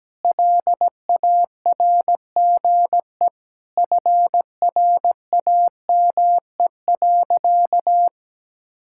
Morse code, heard as a keyed tone, about 17 wpm, 700 Hz. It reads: LARGE FRAME.
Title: Sentences from Top 1000 Words - Review Part 6 - 17wpm